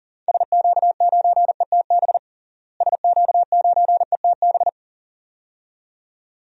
Morse code, heard as a keyed tone, about 40 wpm, 700 Hz. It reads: SQ9ETB SQ9ETB
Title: DXCC-100 Call Signs - 6 Characters - Repeated 2x - 40wpm